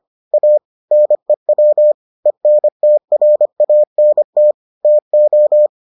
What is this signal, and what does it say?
Morse code, heard as a keyed tone, about 25 wpm, 600 Hz. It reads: A NEW ENTRANT TO